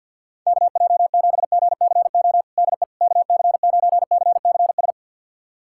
Morse code, heard as a keyed tone, about 49 wpm, 700 Hz. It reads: KJ7GCO DE KC9CCS
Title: USA Callsign DE Callsign 50wpm